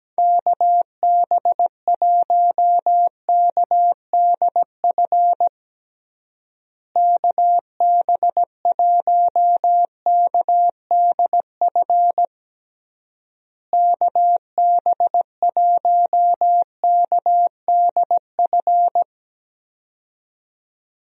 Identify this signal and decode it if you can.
Morse code, heard as a keyed tone, about 17 wpm, 700 Hz. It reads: KB1KDF KB1KDF KB1KDF
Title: USA Call Signs - 6 Characters - Repeated 3x - 17wpm